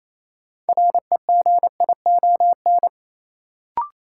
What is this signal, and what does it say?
Morse code, heard as a keyed tone, about 28 wpm, 700 Hz. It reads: REGION
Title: Sentences from Top 1500 Words - New Word Drill Alt - 28wpm